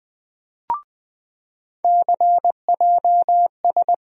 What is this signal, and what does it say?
Morse code, heard as a keyed tone, about 21 wpm, 700 Hz. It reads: CJS
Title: USA Call Sign Postfixes - 3 Letters 20wpm